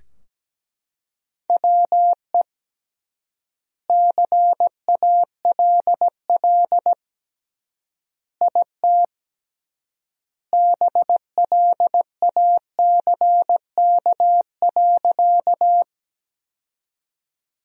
Morse code, heard as a keyed tone, about 17 wpm, 700 Hz. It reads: WE CALL IT BLACK.
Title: 3x Word-Spacing - Sentences from Top 400 Words - 17wpm